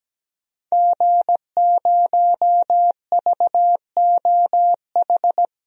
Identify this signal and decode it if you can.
Morse code, heard as a keyed tone, about 17 wpm, 700 Hz. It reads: G0VOH
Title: DXCC-100 Call Signs - 5 Characters - 17wpm